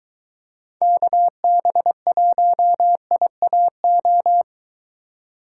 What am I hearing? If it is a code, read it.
Morse code, heard as a keyed tone, about 22 wpm, 700 Hz. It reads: KB1IAO